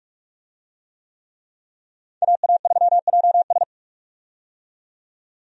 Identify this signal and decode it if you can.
Morse code, heard as a keyed tone, about 43 wpm, 700 Hz. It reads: AA3JS
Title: USA Call Signs - 5 Characters - 45wpm